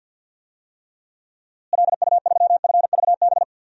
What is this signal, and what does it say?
Morse code, heard as a keyed tone, about 51 wpm, 700 Hz. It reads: LU3FVB
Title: DXCC-100 Call Signs - 6 Characters - Repeated 3x - 50wpm